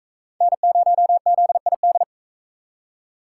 Morse code, heard as a keyed tone, about 41 wpm, 700 Hz. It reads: N0ZID